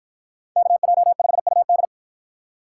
Morse code, heard as a keyed tone, about 51 wpm, 700 Hz. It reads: KJ5UD